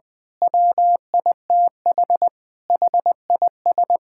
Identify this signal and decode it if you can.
Morse code, heard as a keyed tone, about 20 wpm, 700 Hz. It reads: WITH HIS
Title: Pangram Challenge v2 - 20wpm